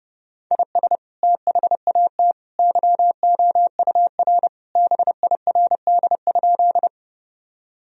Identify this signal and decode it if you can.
Morse code, heard as a keyed tone, about 30 wpm, 700 Hz. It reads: IS THAT YOUR BIRD?
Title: Sentences from Top 1000 Words - Review Part 2 - 30wpm